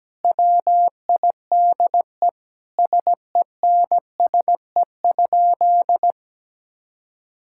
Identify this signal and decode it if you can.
Morse code, heard as a keyed tone, about 17 wpm, 700 Hz. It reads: WIDE SENSE?
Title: Sentences from Top 1000 Words - Review Part 6 - 17wpm